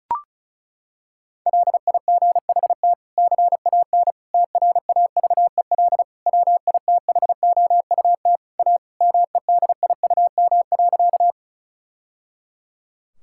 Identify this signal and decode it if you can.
Morse code, heard as a keyed tone, about 34 wpm, 700 Hz. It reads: LIGHT CAN TRAVEL WITHOUT A MEDIUM.
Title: Sentences from Top 1000 Words - Review Part 6 - 35wpm